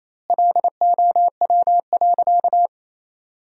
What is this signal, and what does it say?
Morse code, heard as a keyed tone, about 28 wpm, 700 Hz. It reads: LOW.
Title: 2x Word-Spacing - Sentences from Top 500 Words - 28wpm